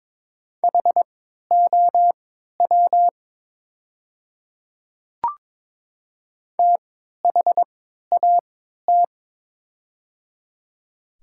Morse code, heard as a keyed tone, about 22 wpm, spaced as for 12 wpm, 700 Hz. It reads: HOW THAT
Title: 3x Character Spacing - Top 100 Words - 22wpm